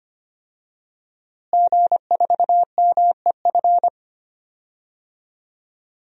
Morse code, heard as a keyed tone, about 25 wpm, 700 Hz. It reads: G4MEF
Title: DXCC-100 Call Signs - 5 Characters - Repeated 2x - 25wpm